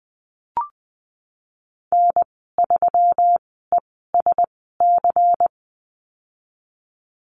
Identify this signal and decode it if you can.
Morse code, heard as a keyed tone, about 20 wpm, spaced as for 15 wpm, 700 Hz. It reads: N3ESC
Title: 245 Lesson 39g - Letter z - Call Signs 20wpm